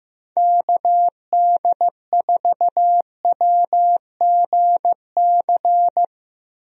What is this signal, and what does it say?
Morse code, heard as a keyed tone, about 16 wpm, 700 Hz. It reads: KD4WGC
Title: USA Call Signs - 6 Characters - Repeated 3x - 15wpm